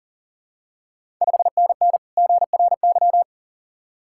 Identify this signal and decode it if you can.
Morse code, heard as a keyed tone, about 40 wpm, 700 Hz. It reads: 5NN GRY
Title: Ontario QSO Party Exchange 40wpm